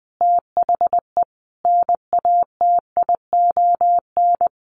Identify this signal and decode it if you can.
Morse code, heard as a keyed tone, about 20 wpm, 700 Hz. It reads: THE NATION
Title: Sets of 5 Words Encore2 20wpm